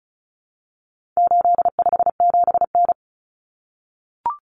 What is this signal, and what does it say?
Morse code, heard as a keyed tone, about 34 wpm, 700 Hz. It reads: 857N
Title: Common Cut Numbers - 4-Digit 35wpm